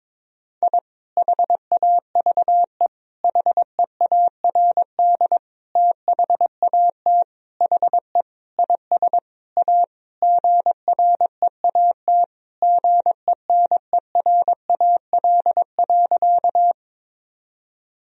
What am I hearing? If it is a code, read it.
Morse code, heard as a keyed tone, about 22 wpm, 700 Hz. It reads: I HAVE HEARD THAT HE IS A GREAT GENERAL.
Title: Sentences from Top 1000 Words - Review Part 3 - 22wpm